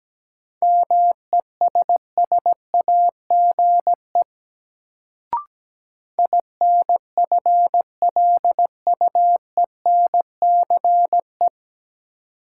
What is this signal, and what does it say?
Morse code, heard as a keyed tone, about 17 wpm, 700 Hz. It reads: MESSAGE INFLUENCE
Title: Sentences from Top 1100 Words - New Word Drill - 17wpm